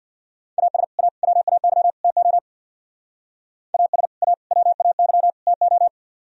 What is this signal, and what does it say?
Morse code, heard as a keyed tone, about 60 wpm, 700 Hz. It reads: AS A WAY TO AS A WAY TO
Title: Speed-Racing - Sets of 4 Words 50wpm